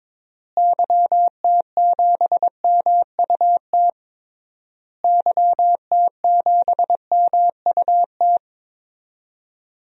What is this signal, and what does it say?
Morse code, heard as a keyed tone, about 22 wpm, 700 Hz. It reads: YT7MUT YT7MUT
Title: DXCC-100 Call Signs - 6 Characters - Repeated 3x - 22wpm